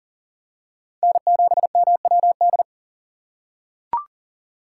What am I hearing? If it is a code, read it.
Morse code, heard as a keyed tone, about 40 wpm, 700 Hz. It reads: N7MWD
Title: USA Call Signs - 5 Characters - 2x Character Spacing - 40wpm